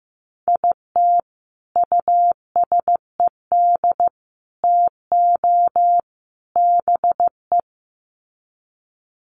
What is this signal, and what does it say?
Morse code, heard as a keyed tone, about 15 wpm, 700 Hz. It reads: IT USED TO BE
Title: Sets of 4 Words 15wpm